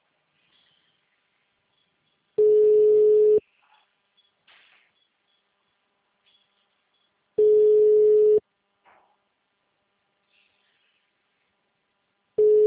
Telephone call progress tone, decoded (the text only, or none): ringback tone